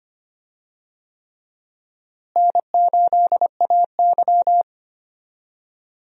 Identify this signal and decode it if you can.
Morse code, heard as a keyed tone, about 25 wpm, 700 Hz. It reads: N8AY
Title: USA Call Signs - 4 Characters - Repeated 3x - 25wpm